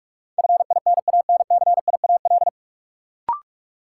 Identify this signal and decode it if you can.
Morse code, heard as a keyed tone, about 41 wpm, 700 Hz. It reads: FINANCIAL
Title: Sentences from Top 1000 Words - Review Part 5 - New Word Drill - 45wpm